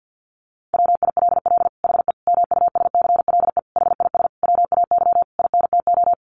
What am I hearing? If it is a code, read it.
Morse code, heard as a keyed tone, about 50 wpm, 700 Hz. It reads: WILL HE MUSCLE HIS WAY INTO
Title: Sentences from Top 1300 Words 50wpm